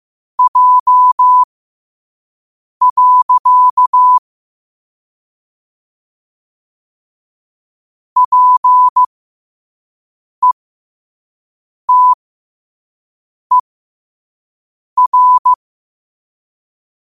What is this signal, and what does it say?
Morse code, heard as a keyed tone, about 15 wpm, spaced as for 5 wpm, 1000 Hz. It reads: J. PETER